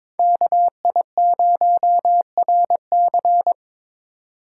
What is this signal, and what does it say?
Morse code, heard as a keyed tone, about 26 wpm, 700 Hz. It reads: KI0RC
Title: Speed Racing - USA 2x2 Call Signs 22wpm